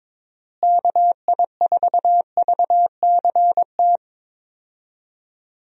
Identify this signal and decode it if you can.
Morse code, heard as a keyed tone, about 22 wpm, 700 Hz. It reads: KI4VCT
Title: USA Call Signs - 6 Characters - Repeated 3x - 22wpm